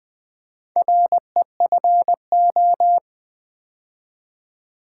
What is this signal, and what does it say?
Morse code, heard as a keyed tone, about 20 wpm, 700 Hz. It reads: REFO
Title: Build A Word - 2 Letters At A Time - 6-Letter Words - 20wpm